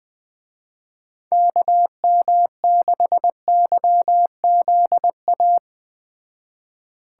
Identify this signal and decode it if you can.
Morse code, heard as a keyed tone, about 20 wpm, 700 Hz. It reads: KM6YZA